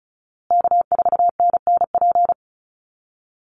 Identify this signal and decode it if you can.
Morse code, heard as a keyed tone, about 34 wpm, 700 Hz. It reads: K4NNP
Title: USA Call Signs Repeated 3x 35wpm